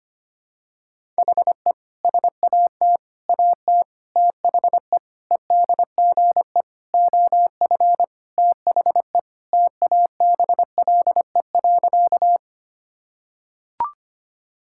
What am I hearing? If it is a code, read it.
Morse code, heard as a keyed tone, about 25 wpm, 700 Hz. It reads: HE SAT AT THE EDGE OF THE TABLE.
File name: practice / 25wpm / Sentences with 3-Word Phrases - 25wpm